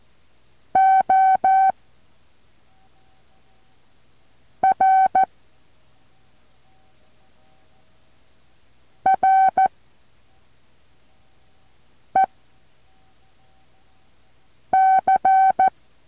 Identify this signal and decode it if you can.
Morse code, heard as a keyed tone, about 14 wpm, 760 Hz. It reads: O R R E C
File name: intl morse code2